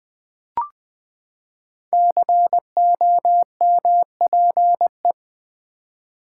Morse code, heard as a keyed tone, about 20 wpm, 700 Hz. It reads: COMPE
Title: Build A Word - 5 Letters At A Time - 10-Letter Words - 20wpm